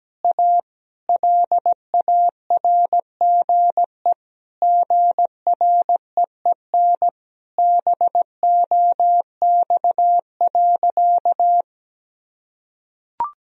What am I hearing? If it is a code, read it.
Morse code, heard as a keyed tone, about 17 wpm, 700 Hz. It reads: A LARGE GREEN BOX.
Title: Sentences from Top 1000 Words - Review Part 2 - 17wpm